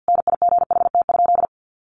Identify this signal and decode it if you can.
Morse code, heard as a keyed tone, about 50 wpm, 700 Hz. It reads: NIGHT?
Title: Sentences from Top 1300 Words 50wpm